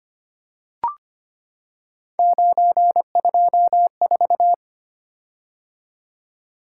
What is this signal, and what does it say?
Morse code, heard as a keyed tone, about 25 wpm, 700 Hz. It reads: 924
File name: Three Letter-Number Combinations - 25wpm